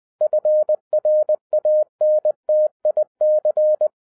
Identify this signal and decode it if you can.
Morse code, heard as a keyed tone, about 20 wpm, 610 Hz. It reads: FRANTIC